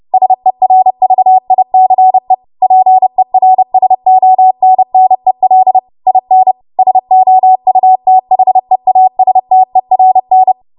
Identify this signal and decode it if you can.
Morse code, heard as a keyed tone, about 30 wpm, 750 Hz. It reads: SERVICE PERSONNEL IN SOUTHEASTERN
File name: Archive / 30 WPM / 30WPM